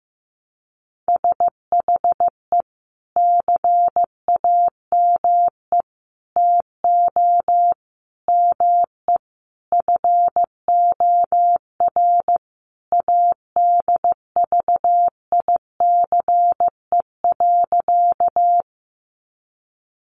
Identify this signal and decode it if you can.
Morse code, heard as a keyed tone, about 15 wpm, 700 Hz. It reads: SHE CAME TO ME FOR ADVICE.
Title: Sentences from Top 1100 Words 15wpm